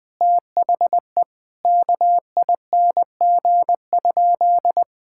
Morse code, heard as a keyed tone, about 20 wpm, 700 Hz. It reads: THE KING?
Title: Sentences from Top 1000 Words - Review Part 5 - 20wpm